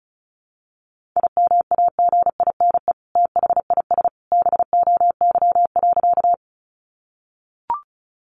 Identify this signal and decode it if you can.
Morse code, heard as a keyed tone, about 34 wpm, 700 Hz. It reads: IMAGINE THIS BOY.
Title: Sentences from Top 1400 Words 35wpm